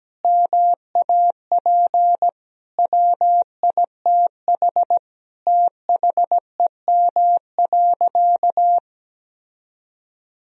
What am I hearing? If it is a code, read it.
Morse code, heard as a keyed tone, about 17 wpm, 700 Hz. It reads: MAP WITH THEM.
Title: Sentences from Top 1000 Words - Review Part 2 - 17wpm